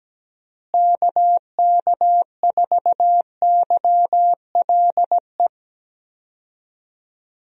Morse code, heard as a keyed tone, about 17 wpm, 700 Hz. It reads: KK4YLE
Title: USA Call Signs - 6 Characters - Repeated 2x - 17wpm